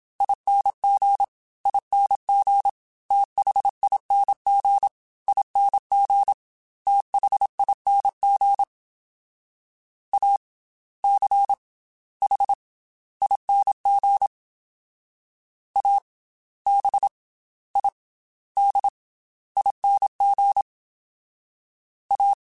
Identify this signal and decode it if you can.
Morse code, heard as a keyed tone, about 27 wpm, 800 Hz. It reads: ING ING THING ING THING A C H ING A B I D ING A
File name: Special INGs / Special INGs-mp3s / Slow ing @25